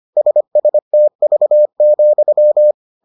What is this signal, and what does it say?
Morse code, heard as a keyed tone, about 25 wpm, 600 Hz. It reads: SSTV,